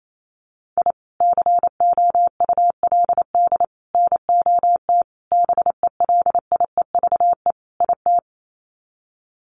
Morse code, heard as a keyed tone, about 28 wpm, 700 Hz. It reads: I COULD NOT BELIEVE IT
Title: Sets of 5 Words Encore 28wpm